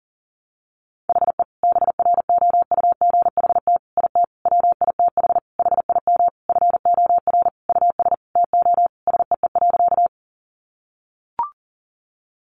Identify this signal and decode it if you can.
Morse code, heard as a keyed tone, about 40 wpm, 700 Hz. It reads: HE BROUGHT IT WITH HIM FOR US TO SEE.
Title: Sentences from Top 500 Words 40wpm